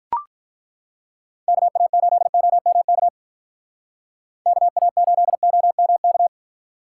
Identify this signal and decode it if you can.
Morse code, heard as a keyed tone, about 42 wpm, 700 Hz. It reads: KA8OMK KA8OMK
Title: Speed Racing - USA 2x3 Call Signs 35wpm